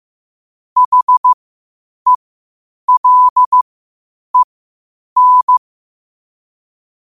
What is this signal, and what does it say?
Morse code, heard as a keyed tone, about 15 wpm, spaced as for 8 wpm, 1000 Hz. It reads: HELEN
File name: qotdfiles / 8wpm